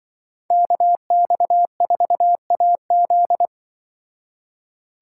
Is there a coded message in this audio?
Yes: Morse code from 0.5 to 3.5 s, heard as a keyed tone, about 25 wpm, 700 Hz. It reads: KX4AZ